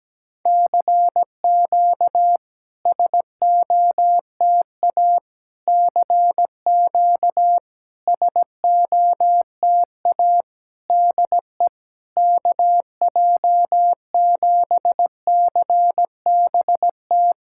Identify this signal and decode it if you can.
Morse code, heard as a keyed tone, about 17 wpm, 690 Hz. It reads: CQ SOTA CQ SOTA DE KJ7CBT
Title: SOTA QSO Practice 17wpm